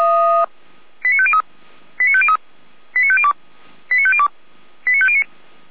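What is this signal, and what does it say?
Five-tone selective call (ZVEI), heard as quick runs of five tones, tones of about 70 ms. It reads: 89503 89503 89502 89502 89509
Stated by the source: ZVEI five-tone selective call 89250 received by SDR, from the Feuerwehr Orlamünde audio archive.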